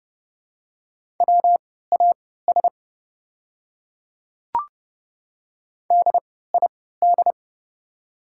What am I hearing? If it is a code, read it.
Morse code, heard as a keyed tone, about 30 wpm, spaced as for 17 wpm, 700 Hz. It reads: WAS DID
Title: 3x Character Spacing - Top 100 Words - 30wpm